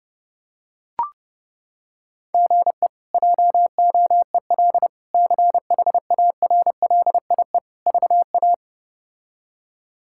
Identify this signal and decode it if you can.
Morse code, heard as a keyed tone, about 30 wpm, 700 Hz. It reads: GE JOEL CHARLIE VA
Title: K1USN SST QSO Practice - 30wpm